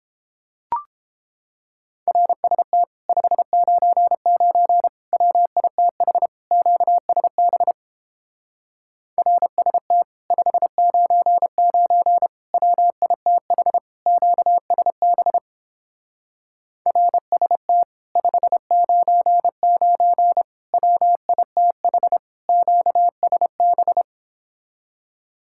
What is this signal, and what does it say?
Morse code, heard as a keyed tone, about 29 wpm, 700 Hz. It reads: RST 599 WITH QSB RST 599 WITH QSB RST 599 WITH QSB